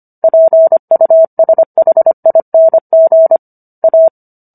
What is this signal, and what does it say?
Morse code, heard as a keyed tone, about 25 wpm, 650 Hz. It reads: PUSHING A